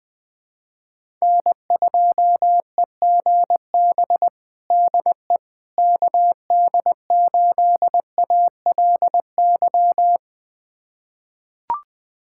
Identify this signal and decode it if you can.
Morse code, heard as a keyed tone, about 20 wpm, 700 Hz. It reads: N2EGB DE KD8ALY
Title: USA Callsign DE Callsign 20wpm